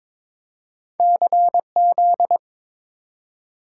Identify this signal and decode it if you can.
Morse code, heard as a keyed tone, about 22 wpm, 700 Hz. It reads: CZ
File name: practice / 22wpm / Two Letter-Number Combinations - 22wpm